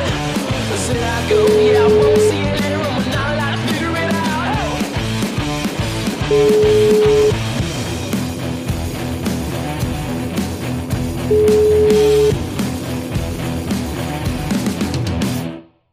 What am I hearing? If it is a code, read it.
Ringback tone (Europe, 425 Hz), with noise.